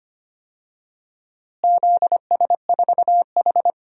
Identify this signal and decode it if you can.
Morse code, heard as a keyed tone, about 25 wpm, 700 Hz. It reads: ZS4H
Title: DXCC-100 Call Signs - 4 Characters - Repeated 3x - 25wpm